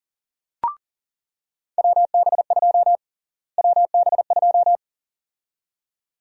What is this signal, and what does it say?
Morse code, heard as a keyed tone, about 40 wpm, 700 Hz. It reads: WB2 WB2
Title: USA Call Sign Prefixes - Repeated 2x - 40wpm